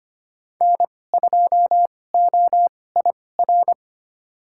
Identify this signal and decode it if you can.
Morse code, heard as a keyed tone, about 25 wpm, spaced as for 18 wpm, 700 Hz. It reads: N2OIR
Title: USA Call Signs - 5 Characters - 2x Character Spacing - 25wpm